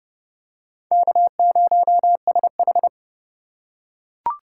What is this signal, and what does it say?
Morse code, heard as a keyed tone, about 30 wpm, 700 Hz. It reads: K0SH
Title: USA Call Signs - 4 Characters - 4x Character Spacing - 30wpm